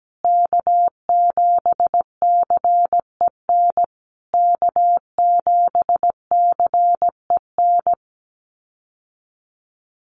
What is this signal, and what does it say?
Morse code, heard as a keyed tone, about 17 wpm, 700 Hz. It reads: K7CEN K7CEN
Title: USA Call Signs Repeated 3x 17wpm